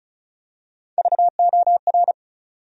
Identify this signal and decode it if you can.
Morse code, heard as a keyed tone, about 36 wpm, 700 Hz. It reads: VOR
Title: USA Call Sign Postfixes - 3 Letters 35wpm